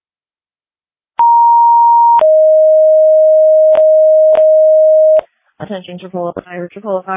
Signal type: two-tone page